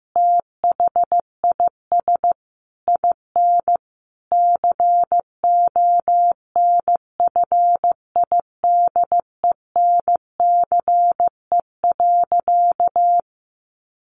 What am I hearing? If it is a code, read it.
Morse code, heard as a keyed tone, about 14 wpm, 700 Hz. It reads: THIS IN CONFIDENCE.
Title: Sentences from Top 1500 Words 15wpm